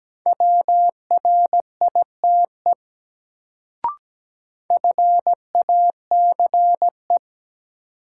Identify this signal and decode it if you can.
Morse code, heard as a keyed tone, about 17 wpm, 700 Hz. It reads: WRITE FACE
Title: Sentences from Top 300 Words - New Word Drill - 17wpm